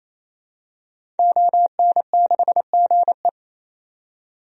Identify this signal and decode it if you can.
Morse code, heard as a keyed tone, about 28 wpm, 700 Hz. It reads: ON6GE
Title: DXCC-100 Call Signs - 5 Characters - 28wpm